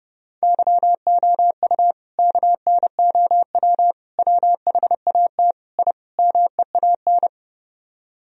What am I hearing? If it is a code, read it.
Morse code, heard as a keyed tone, about 30 wpm, 700 Hz. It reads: YOU KNOW WHAT I MEAN